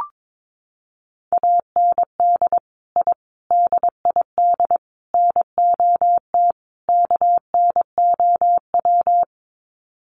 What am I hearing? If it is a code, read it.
Morse code, heard as a keyed tone, about 22 wpm, 700 Hz. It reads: AND I DID NOT KNOW
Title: Sets of 5 Words Encore 22wpm